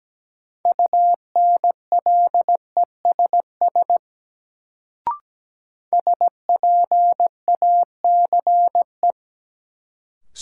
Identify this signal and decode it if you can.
Morse code, heard as a keyed tone, about 17 wpm, 700 Hz. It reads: UNLESS SPACE
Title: Sentences from Top 1000 Words - Review Part 5 - New Word Drill - 17wpm